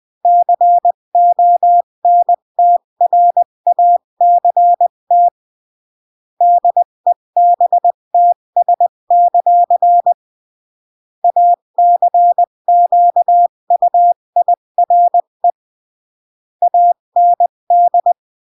Morse code, heard as a keyed tone, about 20 wpm, 700 Hz. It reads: CONTRACT DEBTS; ACQUIRE AND
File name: CW / jodorowsky / 82 commandments 20-15